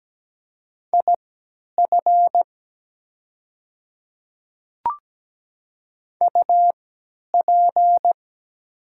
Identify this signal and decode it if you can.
Morse code, heard as a keyed tone, about 17 wpm, spaced as for 10 wpm, 700 Hz. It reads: IF UP
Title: 3x Character Spacing - Top 300 Words - 17wpm